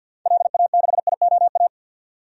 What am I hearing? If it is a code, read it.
Morse code, heard as a keyed tone, about 52 wpm, 700 Hz. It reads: LA6IOA
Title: DXCC-100 Call Signs - 6 Characters - 50wpm